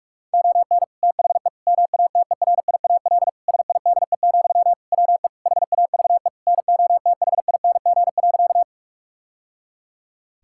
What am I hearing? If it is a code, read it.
Morse code, heard as a keyed tone, about 45 wpm, 700 Hz. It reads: ON THE MATERIAL SIDE, WE HAVE NOTHING.
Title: Sentences from Top 1000 Words - Review Part 5 - 45wpm